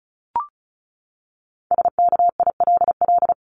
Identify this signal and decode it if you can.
Morse code, heard as a keyed tone, about 34 wpm, 700 Hz. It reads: SKILL
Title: Top 1001-1250 Words 35wpm